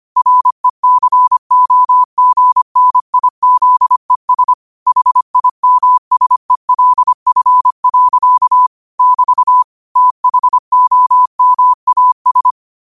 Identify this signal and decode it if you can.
Morse code, heard as a keyed tone, about 25 wpm, 1000 Hz. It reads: RECOGNIZES HIMSELF. <BT> THOMAS